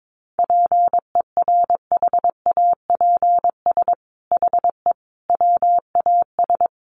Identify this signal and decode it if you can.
Morse code, heard as a keyed tone, about 22 wpm, 700 Hz. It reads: PERHAPS HE WAS
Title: Sentences from Top 1100 Words 22wpm